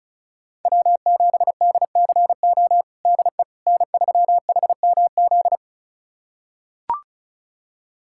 Morse code, heard as a keyed tone, about 34 wpm, 690 Hz. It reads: W7DCO DE N3HMZ
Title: USA Callsign DE Callsign - 5 Characters - 35wpm